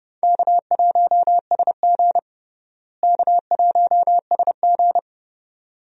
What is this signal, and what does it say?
Morse code, heard as a keyed tone, about 31 wpm, 700 Hz. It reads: K1SG K1SG
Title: USA Call Signs - 4 Characters - Repeated 2x - 30wpm